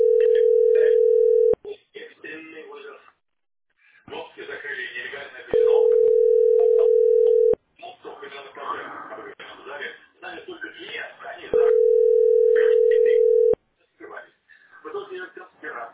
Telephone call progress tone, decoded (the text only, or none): ringback tone